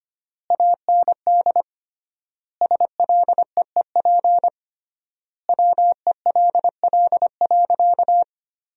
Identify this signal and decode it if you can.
Morse code, heard as a keyed tone, about 25 wpm, 700 Hz. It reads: AND SLEEP WELL.
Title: 3x Word-Spacing - Sentences from Top 500 Words - 25wpm